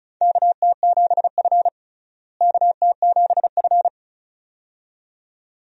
Morse code, heard as a keyed tone, about 36 wpm, 700 Hz. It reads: KT7F KT7F
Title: USA Call Signs - 4 Characters - Repeated 2x - 35wpm